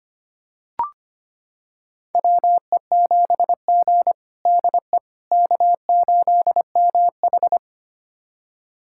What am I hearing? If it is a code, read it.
Morse code, heard as a keyed tone, about 25 wpm, 700 Hz. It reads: WE7G DE K8MH
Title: USA Callsign DE Callsign - 4 Characters - 25wpm